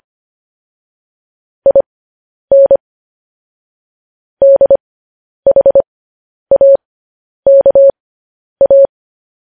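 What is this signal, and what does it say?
Morse code, heard as a keyed tone, about 25 wpm, spaced as for 10 wpm, 550 Hz. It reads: IN DHAKA
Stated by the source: Morse news headlines at 10.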